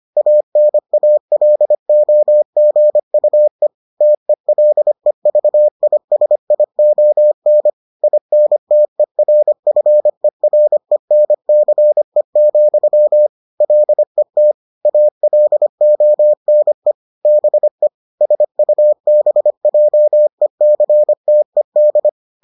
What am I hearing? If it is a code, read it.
Morse code, heard as a keyed tone, about 25 wpm, 600 Hz. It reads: ANALOGUE TELEVISION INTERFERENCE, LET ALONE BE SUBJECTED